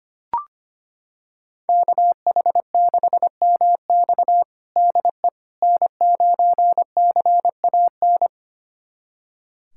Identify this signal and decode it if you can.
Morse code, heard as a keyed tone, about 25 wpm, 700 Hz. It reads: KH6MX DE N9CAN
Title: USA Callsign DE Callsign - 5 Characters - 25wpm